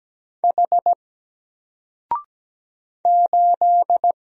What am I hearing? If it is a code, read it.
Morse code, heard as a keyed tone, about 17 wpm, 700 Hz. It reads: H 8
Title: Single Letter-Number 17wpm